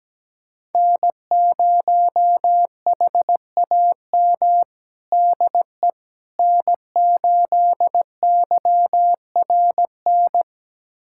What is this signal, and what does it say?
Morse code, heard as a keyed tone, about 17 wpm, 700 Hz. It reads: N0HAM DE N8YRN
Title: USA Callsign DE Callsign - 5 Characters - 17wpm